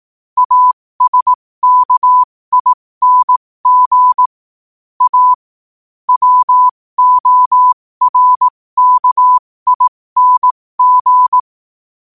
Morse code, heard as a keyed tone, about 18 wpm, spaced as for 15 wpm, 1000 Hz. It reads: ASKING A WORKING
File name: qotdfiles / 15wpm